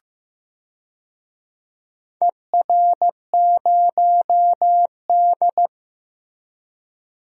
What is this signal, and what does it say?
Morse code, heard as a keyed tone, about 15 wpm, 700 Hz. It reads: ER0D
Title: DXCC-100 Call Signs - 4 Characters - Repeated 3x - 15wpm